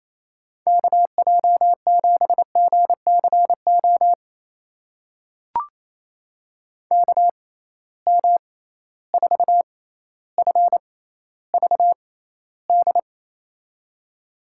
Morse code, heard as a keyed tone, about 28 wpm, spaced as for 14 wpm, 700 Hz. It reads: KJ7GCO KM4FVD
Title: USA Call Signs - 6 Characters - 6x Character Spacing - 28wpm